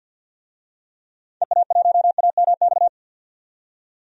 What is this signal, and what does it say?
Morse code, heard as a keyed tone, about 51 wpm, 700 Hz. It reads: EA1AMX
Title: DXCC-100 Call Signs - 6 Characters - 50wpm